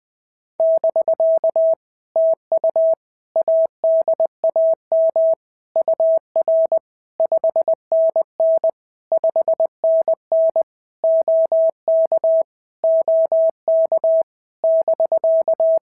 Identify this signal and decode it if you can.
Morse code, heard as a keyed tone, about 20 wpm, 650 Hz. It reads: <BK> TU ADAM UR 5NN 5NN OK OK <BK>
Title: POTA QSO Practice - 20wpm